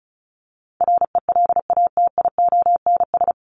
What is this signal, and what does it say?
Morse code, heard as a keyed tone, about 34 wpm, 700 Hz. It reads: RELATIONS
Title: ES:S Suffix - Top 100 Words with Base Words 35wpm